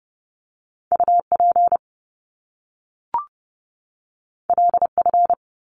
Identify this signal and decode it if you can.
Morse code, heard as a keyed tone, about 30 wpm, 700 Hz. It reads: UP LF
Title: Top Two Letter Combinations in Words 30wpm